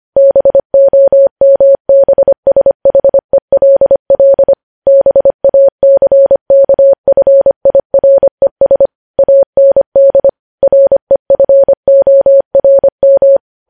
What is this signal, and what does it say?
Morse code, heard as a keyed tone, about 25 wpm, 550 Hz. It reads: BOMBSHELL BACKFIRES AND REFORM
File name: podcast / News headlines-1x-25x25